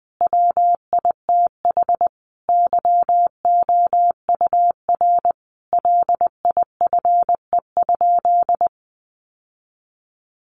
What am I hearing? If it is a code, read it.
Morse code, heard as a keyed tone, about 19 wpm, 700 Hz. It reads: WITH YOUR LIFE?
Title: Sentences from Top 900 Words 20wpm